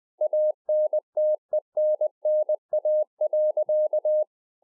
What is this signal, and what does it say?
Morse code, heard as a keyed tone, about 20 wpm, 610 Hz. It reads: ANTENNA.